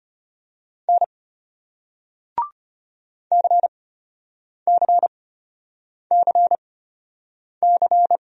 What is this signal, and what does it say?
Morse code, heard as a keyed tone, about 34 wpm, spaced as for 8 wpm, 700 Hz. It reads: N CCCC